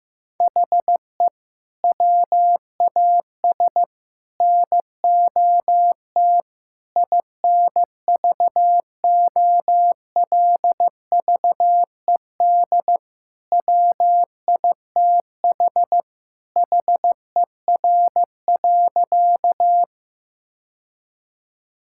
Morse code, heard as a keyed tone, about 15 wpm, 700 Hz. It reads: HE WAS NOT INVOLVED WITH HER.
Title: Sentences from Top 1000 Words - Review Part 5 - 15wpm